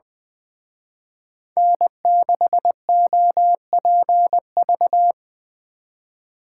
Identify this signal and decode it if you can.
Morse code, heard as a keyed tone, about 24 wpm, 700 Hz. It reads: N6OPV
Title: Speed Racing - USA 1x3 Call Signs 20wpm